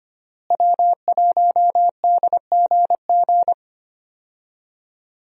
Morse code, heard as a keyed tone, about 25 wpm, 700 Hz. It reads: W1DGG
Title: USA Call Signs - 5 Characters - Repeated 3x - 25wpm